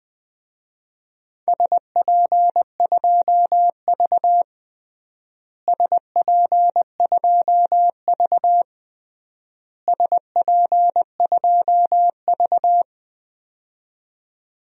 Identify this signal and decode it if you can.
Morse code, heard as a keyed tone, about 20 wpm, 700 Hz. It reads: SP2V SP2V SP2V
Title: DXCC-100 Call Signs - 4 Characters - Repeated 3x - 20wpm